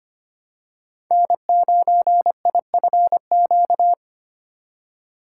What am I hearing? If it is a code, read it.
Morse code, heard as a keyed tone, about 25 wpm, 700 Hz. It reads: N9IFQ
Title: USA Call Signs - 5 Characters - 4x Character Spacing - 25wpm